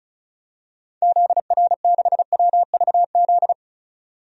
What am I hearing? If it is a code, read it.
Morse code, heard as a keyed tone, about 36 wpm, 700 Hz. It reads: ZR6WVZ
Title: DXCC-100 Call Signs - 6 Characters - 35wpm